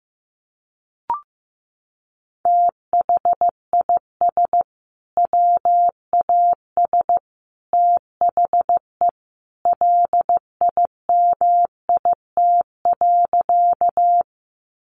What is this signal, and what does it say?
Morse code, heard as a keyed tone, about 15 wpm, 700 Hz. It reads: THIS WAS THE LIMIT.
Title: Sentences from Top 1500 Words 15wpm